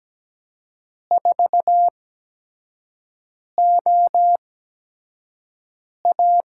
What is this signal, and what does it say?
Morse code, heard as a keyed tone, about 17 wpm, spaced as for 5 wpm, 700 Hz. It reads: 4OA
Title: USA Call Signs - 4 Characters - 8x Character Spacing - 17wpm